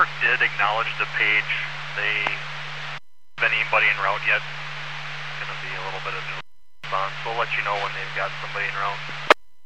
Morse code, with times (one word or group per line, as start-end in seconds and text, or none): none